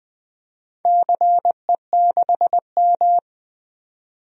Morse code, heard as a keyed tone, about 20 wpm, 700 Hz. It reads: CE6M
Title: DXCC-100 Call Signs - 4 Characters - 20wpm